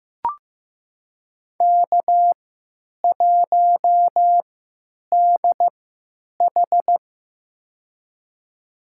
Morse code, heard as a keyed tone, about 15 wpm, spaced as for 9 wpm, 700 Hz. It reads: K1DH
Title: USA Call Signs - 4 Characters - 3x Character Spacing - 15wpm